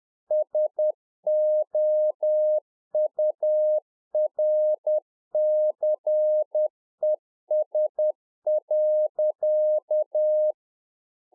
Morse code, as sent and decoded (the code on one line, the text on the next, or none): ... --- ..- .-. -.-. ... .-.-.-
SOURCES.